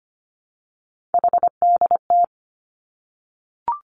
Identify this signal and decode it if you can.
Morse code, heard as a keyed tone, about 25 wpm, 700 Hz. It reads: HDT
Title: Three Challenging Letter-Number Combinations 25wpm